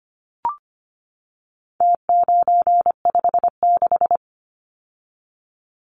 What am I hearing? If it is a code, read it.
Morse code, heard as a keyed tone, about 25 wpm, 700 Hz. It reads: T956
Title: Numbers - Contest Style - 4-Digit 25wpm